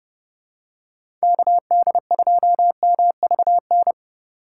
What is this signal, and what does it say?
Morse code, heard as a keyed tone, about 29 wpm, 700 Hz. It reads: KD2MVN